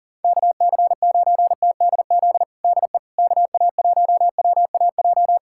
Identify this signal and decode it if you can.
Morse code, heard as a keyed tone, about 40 wpm, 700 Hz. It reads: KC9TDZ DE KA1WAJ
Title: USA Callsign DE Callsign - 6 Characters - 40wpm